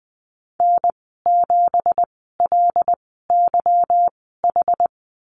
Morse code, heard as a keyed tone, about 19 wpm, spaced as for 14 wpm, 700 Hz. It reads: N7LYH
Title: 179 Lesson 29g - Slash - Call Signs 20wpm